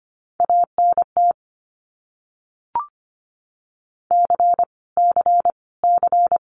Morse code, heard as a keyed tone, about 25 wpm, 700 Hz. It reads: ANT C C C
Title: Abbreviations Repeated 3x - 25wpm